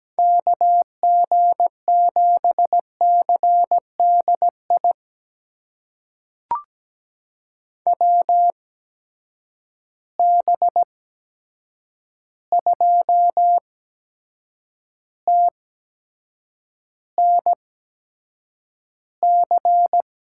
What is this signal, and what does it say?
Morse code, heard as a keyed tone, about 17 wpm, spaced as for 7 wpm, 700 Hz. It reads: KG7CDI WB2TNC
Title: USA Call Signs - 6 Characters - 8x Character Spacing - 17wpm